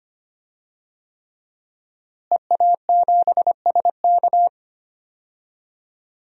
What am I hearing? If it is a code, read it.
Morse code, heard as a keyed tone, about 25 wpm, 700 Hz. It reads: EA7SK
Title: DXCC-100 Call Signs - 5 Characters - Repeated 3x - 25wpm